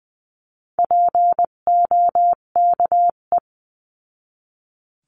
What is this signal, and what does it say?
Morse code, heard as a keyed tone, about 20 wpm, 700 Hz. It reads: POKE
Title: 258 Lesson 42c - Words 16-20wpm